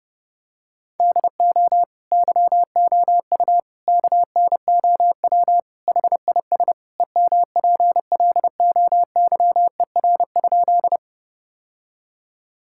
Morse code, heard as a keyed tone, about 30 wpm, 700 Hz. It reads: DO YOU KNOW HIS EMPLOYER?
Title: Sentences from Top 1600 Words 30wpm